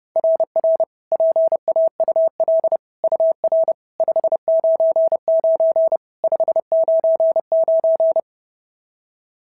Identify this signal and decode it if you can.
Morse code, heard as a keyed tone, about 30 wpm, 660 Hz. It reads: RR PAUL UR 599 599
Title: SOTA QSO Practice 30wpm